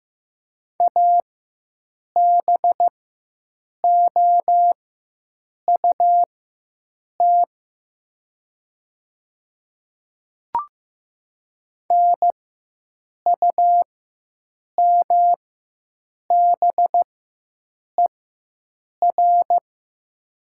Morse code, heard as a keyed tone, about 15 wpm, spaced as for 7 wpm, 700 Hz. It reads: ABOUT NUMBER
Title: 4x Character Spacing - Top 100 Words - 15wpm